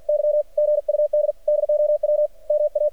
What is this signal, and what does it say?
Morse code, heard as a keyed tone, about 53 wpm, 600 Hz. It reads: Y GUD DOW MA
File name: archived / CRSnet